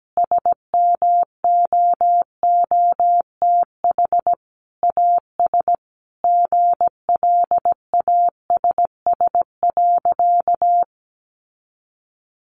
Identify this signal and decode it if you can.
Morse code, heard as a keyed tone, about 17 wpm, 700 Hz. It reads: SMOOTH AS GLASS.
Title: Sentences from Top 1400 Words 17wpm